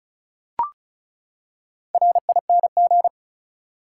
Morse code, heard as a keyed tone, about 36 wpm, 700 Hz. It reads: RING
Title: Four Letter Words - 35wpm